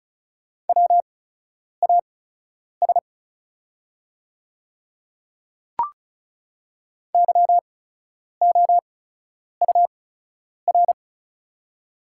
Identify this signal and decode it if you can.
Morse code, heard as a keyed tone, about 36 wpm, spaced as for 10 wpm, 700 Hz. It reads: WAS YOUR
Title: 8x Character Spacing - Top 100 Words - 35wpm